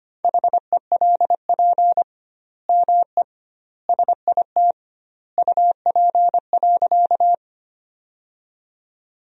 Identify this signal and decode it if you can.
Morse code, heard as a keyed tone, about 25 wpm, 700 Hz. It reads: HELP ME SIT UP.
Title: 2x Word-Spacing - Sentences from Top 500 Words - 25wpm